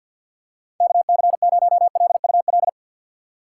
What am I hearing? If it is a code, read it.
Morse code, heard as a keyed tone, about 48 wpm, 690 Hz. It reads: KC0LUL
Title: USA Call Signs - 6 Characters - 3x Character Spacing - 50wpm